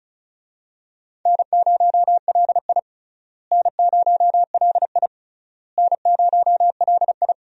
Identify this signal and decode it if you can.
Morse code, heard as a keyed tone, about 36 wpm, 700 Hz. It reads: N0LI N0LI N0LI
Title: USA Call Signs - 4 Characters - Repeated 3x - 35wpm